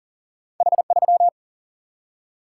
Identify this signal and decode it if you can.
Morse code, heard as a keyed tone, about 39 wpm, 700 Hz. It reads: H3
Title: Single-Letter Single-Number 40wpm